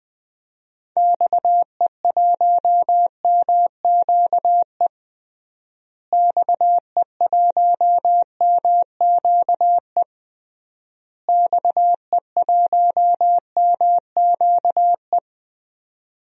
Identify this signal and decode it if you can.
Morse code, heard as a keyed tone, about 20 wpm, 700 Hz. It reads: XE1MQE XE1MQE XE1MQE
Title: DXCC-100 Call Signs - 6 Characters - Repeated 3x - 20wpm